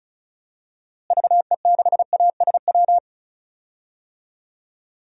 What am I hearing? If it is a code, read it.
Morse code, heard as a keyed tone, about 34 wpm, 700 Hz. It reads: VE6ASW
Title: DXCC-100 Call Signs - 6 Characters - Repeated 2x - 35wpm